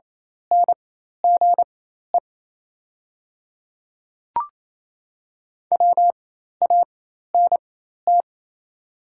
Morse code, heard as a keyed tone, about 28 wpm, spaced as for 13 wpm, 690 Hz. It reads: NGE WANT
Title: 4x Character Spacing - Top 200 Words - 28wpm